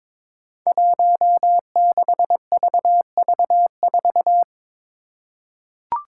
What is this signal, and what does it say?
Morse code, heard as a keyed tone, about 22 wpm, 700 Hz. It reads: 16VV4
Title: Five Challenging Letter-Number Combinations 22wpm